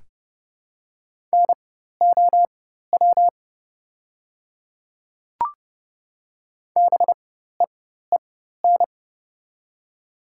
Morse code, heard as a keyed tone, about 30 wpm, spaced as for 14 wpm, 690 Hz. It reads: NOW BEEN
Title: 4x Character Spacing - Top 100 Words - 30wpm